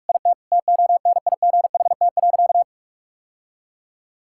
Morse code, heard as a keyed tone, about 46 wpm, 690 Hz. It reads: IT TONIGHT.
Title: Sentences from Top 1000 Words - Review Part 5 - 45wpm